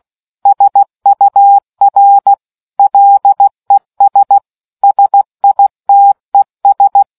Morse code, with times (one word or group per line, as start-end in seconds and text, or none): 0.5-2.3 SUR
2.8-4.4 LES
4.8-7.0 SITES